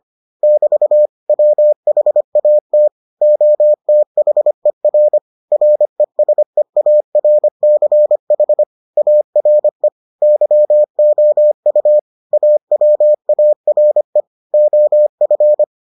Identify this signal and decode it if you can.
Morse code, heard as a keyed tone, about 25 wpm, 600 Hz. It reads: <BT> WHAT OTHER RESEARCH ARE YOU AWARE OF